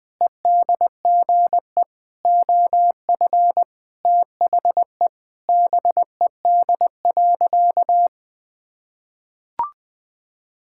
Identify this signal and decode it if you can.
Morse code, heard as a keyed tone, about 20 wpm, 700 Hz. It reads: EDGE OF THE BED.
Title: Sentences with 4-Word Phrases - 20wpm